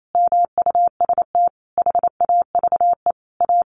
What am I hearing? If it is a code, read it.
Morse code, heard as a keyed tone, about 28 wpm, 700 Hz. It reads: MUST HAVE A
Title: Sentences from Top 500 Words 28wpm